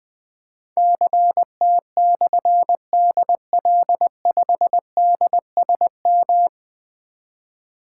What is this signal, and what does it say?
Morse code, heard as a keyed tone, about 20 wpm, 700 Hz. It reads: CT/DL5DSM
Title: Difficult Call Signs 20wpm